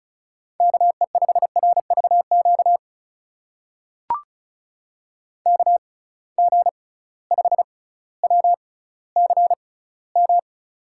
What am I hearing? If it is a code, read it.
Morse code, heard as a keyed tone, about 36 wpm, spaced as for 18 wpm, 700 Hz. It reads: KE5RVQ KG5WCM
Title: USA Call Signs - 6 Characters - 6x Character Spacing - 35wpm